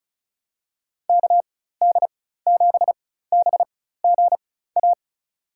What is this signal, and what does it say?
Morse code, heard as a keyed tone, about 36 wpm, spaced as for 16 wpm, 700 Hz. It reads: KD7BGA
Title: USA Call Signs - 6 Characters - 4x Character Spacing - 35wpm